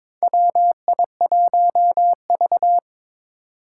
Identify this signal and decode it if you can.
Morse code, heard as a keyed tone, about 22 wpm, 700 Hz. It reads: WI1V
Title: USA Call Signs - 4 Characters - 2x Character Spacing - 22wpm